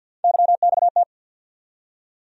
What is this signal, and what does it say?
Morse code, heard as a keyed tone, about 53 wpm, 700 Hz. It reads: YXT
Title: USA Call Sign Postfixes - 3 Letters 50wpm